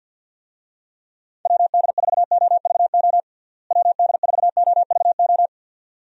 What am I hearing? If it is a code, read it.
Morse code, heard as a keyed tone, about 50 wpm, 700 Hz. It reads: WD4OVO WD4OVO
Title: USA Call Signs - 6 Characters - Repeated 2x - 50wpm